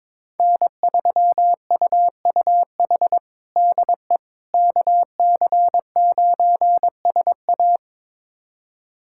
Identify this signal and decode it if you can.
Morse code, heard as a keyed tone, about 22 wpm, 700 Hz. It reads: N3UUH DE KC9SA
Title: USA Callsign DE Callsign - 5 Characters - 22wpm